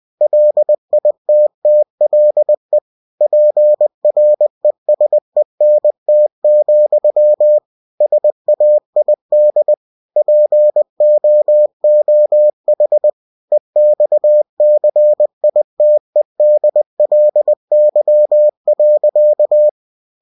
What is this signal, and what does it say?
Morse code, heard as a keyed tone, about 20 wpm, 600 Hz. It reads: LITTLE PRESENT, SAID POOH EXCITEDLY.
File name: cw-books / winnie-the-pooh / winnie-20wpm-page-0057